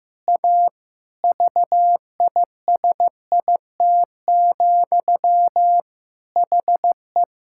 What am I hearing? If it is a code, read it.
Morse code, heard as a keyed tone, about 15 wpm, 700 Hz. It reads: A VISIT, HE
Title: Sentences from Top 1000 Words - Review Part 6 - 15wpm